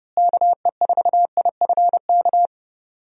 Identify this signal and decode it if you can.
Morse code, heard as a keyed tone, about 30 wpm, 700 Hz. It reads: KE4IFK
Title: USA Call Signs - 6 Characters - Repeated 3x - 30wpm